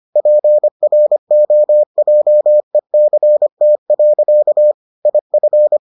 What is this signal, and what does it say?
Morse code, heard as a keyed tone, about 25 wpm, 600 Hz. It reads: PROJECT. IF